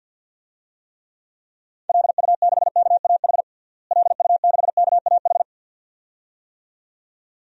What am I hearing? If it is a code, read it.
Morse code, heard as a keyed tone, about 48 wpm, 700 Hz. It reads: LU6KAH LU6KAH
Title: DXCC-100 Call Signs - 6 Characters - Repeated 2x - 50wpm